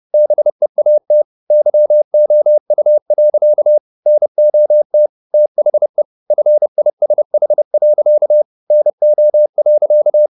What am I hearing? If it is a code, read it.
Morse code, heard as a keyed tone, about 30 wpm, 600 Hz. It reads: BEAT YOU. NOT THE FISH. NO.